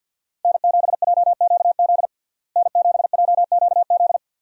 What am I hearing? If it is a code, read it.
Morse code, heard as a keyed tone, about 52 wpm, 700 Hz. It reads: N7JQZ N7JQZ
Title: USA Call Signs - 5 Characters - Repeated 3x - 50wpm